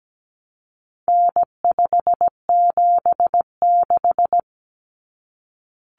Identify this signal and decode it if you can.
Morse code, heard as a keyed tone, about 17 wpm, 700 Hz. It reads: N576
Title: Common Cut Numbers - 4-Digit 17wpm